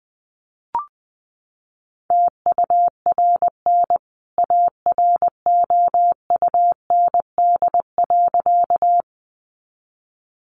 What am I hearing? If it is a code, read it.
Morse code, heard as a keyed tone, about 20 wpm, 700 Hz. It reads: TURN AROUND.
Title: A Sound Of Thunder 20wpm